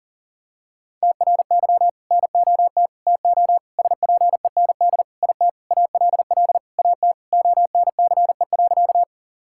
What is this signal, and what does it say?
Morse code, heard as a keyed tone, about 40 wpm, 700 Hz. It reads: TRY NOT TO SPEND IT ALL AT ONCE.
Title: Sentences from Top 1000 Words - Review Part 5 - 40wpm